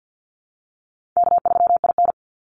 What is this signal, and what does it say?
Morse code, heard as a keyed tone, about 49 wpm, 700 Hz. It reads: K3IN
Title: Call Signs Repeated 3x - 50wpm